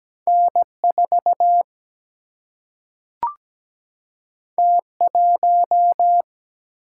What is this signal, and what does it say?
Morse code, heard as a keyed tone, about 17 wpm, 700 Hz. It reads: N 4 T 1
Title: Single-Letter Single-Number 17wpm